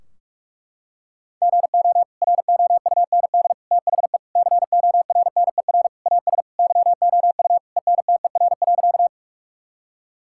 Morse code, heard as a keyed tone, about 46 wpm, 700 Hz. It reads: GO ROUND THE CORNER AS YOU ENTER.
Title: Sentences from Top 1600 Words 45wpm